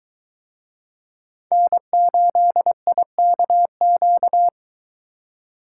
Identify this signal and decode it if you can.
Morse code, heard as a keyed tone, about 22 wpm, 700 Hz. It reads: N8IKQ